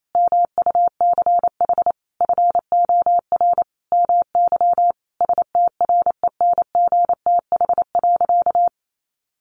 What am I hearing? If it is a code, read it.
Morse code, heard as a keyed tone, about 28 wpm, 700 Hz. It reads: MUCH FOR MY STRENGTH.
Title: Sentences from Top 1300 Words 28wpm